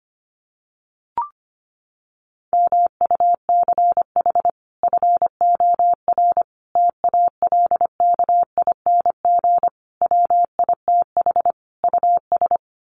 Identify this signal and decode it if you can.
Morse code, heard as a keyed tone, about 25 wpm, 700 Hz. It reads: MUCH FOR TALKING WITH US
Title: Sets of 5 Words Encore 25wpm